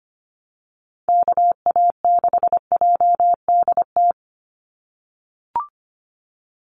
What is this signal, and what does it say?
Morse code, heard as a keyed tone, about 25 wpm, 700 Hz. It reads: KA6JDT
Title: USA Call Signs 25wpm